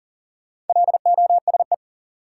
Morse code, heard as a keyed tone, about 40 wpm, 700 Hz. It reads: LOSE
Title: Sentences from Top 1100 Words - New Word Drill - 40wpm